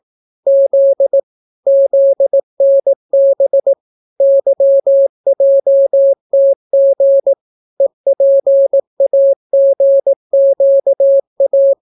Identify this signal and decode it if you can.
Morse code, heard as a keyed tone, about 18 wpm, 550 Hz. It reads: Z ZNB YJTG EPAGQA